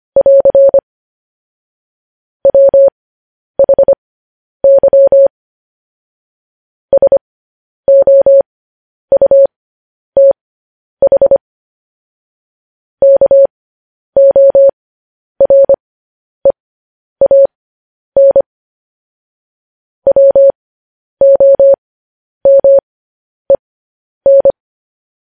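Morse code, heard as a keyed tone, about 25 wpm, spaced as for 10 wpm, 550 Hz. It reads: <AR> WHY SOUTH KOREAN WOMEN
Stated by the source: Morse code news headlines at 10.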